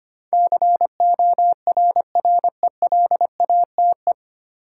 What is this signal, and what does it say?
Morse code, heard as a keyed tone, about 25 wpm, 700 Hz. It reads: CORRELATE
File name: Nine Letter Words - 25wpm